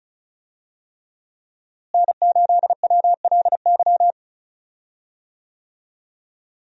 Morse code, heard as a keyed tone, about 36 wpm, 700 Hz. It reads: N8WLY
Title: USA Call Signs - 5 Characters - Repeated 3x - 35wpm